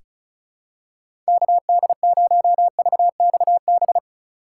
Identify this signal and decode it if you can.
Morse code, heard as a keyed tone, about 36 wpm, 700 Hz. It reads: KD0VXB
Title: USA Call Signs - 6 Characters - 2x Character Spacing - 35wpm